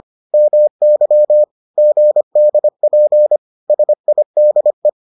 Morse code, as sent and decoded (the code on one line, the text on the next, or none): -- -.-- / --. -.. .--. / ... .. -.. .
MY GDP SIDE